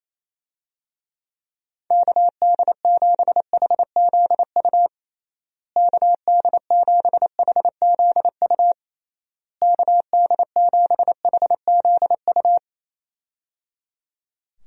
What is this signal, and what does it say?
Morse code, heard as a keyed tone, about 28 wpm, 700 Hz. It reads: KD7HZU KD7HZU KD7HZU
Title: USA Call Signs - 6 Characters - Repeated 3x - 28wpm